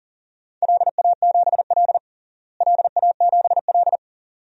Morse code, heard as a keyed tone, about 40 wpm, 700 Hz. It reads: LA7L LA7L
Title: DXCC-100 Call Signs - 4 Characters - Repeated 2x - 40wpm